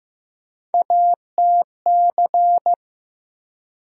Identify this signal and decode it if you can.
Morse code, heard as a keyed tone, about 15 wpm, 700 Hz. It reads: ATC
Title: USA Call Sign Postfixes - 3 Letters 15wpm